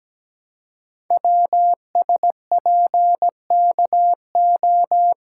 Morse code, heard as a keyed tone, about 17 wpm, 700 Hz. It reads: WSPKO
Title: Code Groups - 5 Letters - 17wpm